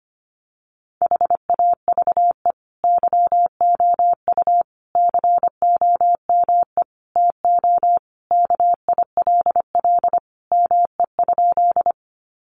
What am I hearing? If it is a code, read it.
Morse code, heard as a keyed tone, about 25 wpm, 700 Hz. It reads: HAVE YOU COME TO KILL ME?
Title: Sentences from Top 1400 Words 25wpm